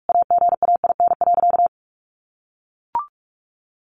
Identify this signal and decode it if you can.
Morse code, heard as a keyed tone, about 44 wpm, 700 Hz. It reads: AGAIN.
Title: Sentences from Top 1500 Words 45wpm